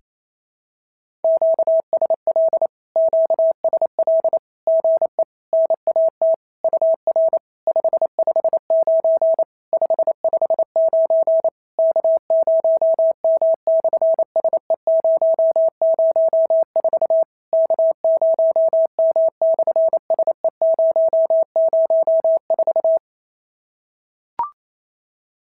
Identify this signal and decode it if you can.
Morse code, heard as a keyed tone, about 27 wpm, 650 Hz. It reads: QSL QSL GE NAT UR 559 559 K0M/SE004 K0M/SE004
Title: SOTA QSO Practice 28wpm